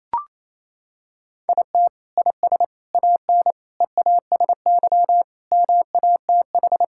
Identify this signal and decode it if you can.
Morse code, heard as a keyed tone, about 28 wpm, 700 Hz. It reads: IT IS AN EASY MATH
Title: Sentences from Top 1000 Words - Review Part 6 - 28wpm